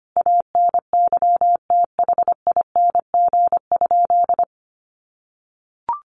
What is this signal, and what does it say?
Morse code, heard as a keyed tone, about 25 wpm, 700 Hz. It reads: ANYTHING?
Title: Sentences from Top 1300 Words 25wpm